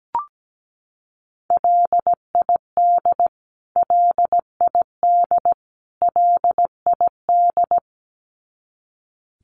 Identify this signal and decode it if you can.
Morse code, heard as a keyed tone, about 17 wpm, 700 Hz. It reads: LID LID LID
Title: Abbreviations Repeated 3x - 17wpm